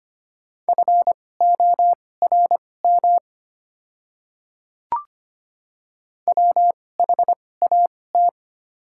Morse code, heard as a keyed tone, about 25 wpm, spaced as for 18 wpm, 700 Hz. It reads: FORM WHAT
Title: 2x Character Spacing - Top 200 Words - 25wpm